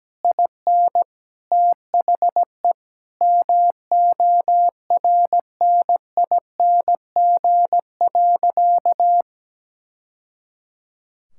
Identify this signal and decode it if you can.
Morse code, heard as a keyed tone, about 17 wpm, 700 Hz. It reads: IN THE MORNING.
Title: Sentences from Top 1000 Words - Review Part 6 - 17wpm